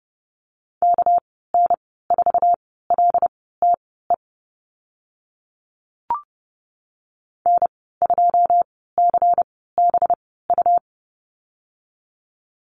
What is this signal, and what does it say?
Morse code, heard as a keyed tone, about 30 wpm, spaced as for 17 wpm, 700 Hz. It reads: KN4LTE N2CBU
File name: Lesson 37g - Letter x - Call Signs 30wpm